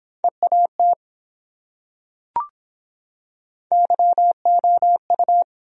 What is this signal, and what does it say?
Morse code, heard as a keyed tone, about 23 wpm, 700 Hz. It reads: EAT YOU